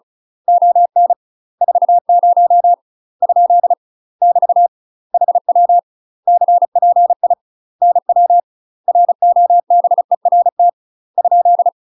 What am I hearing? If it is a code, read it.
Morse code, heard as a keyed tone, about 35 wpm, 700 Hz. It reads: ON 40 ? <BT> HW CPI NW ROBERT ?
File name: UK MASTER QSOs TRAINING / 35WPM / UK MASTER QSOs 0042